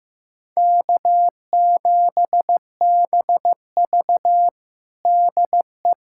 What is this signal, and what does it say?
Morse code, heard as a keyed tone, about 15 wpm, 700 Hz. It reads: K7BV DE
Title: Practice QSO - Exchange 4 - Ending - 15wpm